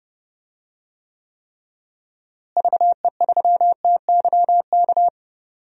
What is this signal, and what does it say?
Morse code, heard as a keyed tone, about 32 wpm, 700 Hz. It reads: VE3TYK
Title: DXCC-100 Call Signs - 6 Characters - Repeated 3x - 30wpm